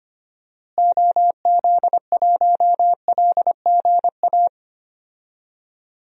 Morse code, heard as a keyed tone, about 25 wpm, 700 Hz. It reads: OZ1LGA
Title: DXCC-100 Call Signs - 6 Characters - Repeated 2x - 25wpm